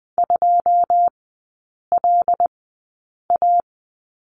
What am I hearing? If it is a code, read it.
Morse code, heard as a keyed tone, about 20 wpm, spaced as for 8 wpm, 690 Hz. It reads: 2LA
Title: 093 Lesson 16e - Letter c - Three Characters 20wpm